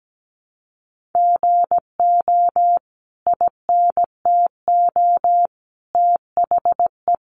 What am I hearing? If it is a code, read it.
Morse code, heard as a keyed tone, about 17 wpm, 700 Hz. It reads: GO INTO THE
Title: Sentences from Top 1500 Words 17wpm